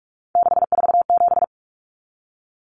Morse code, heard as a keyed tone, about 44 wpm, 700 Hz. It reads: 647
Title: Numbers - 3-Digit 45wpm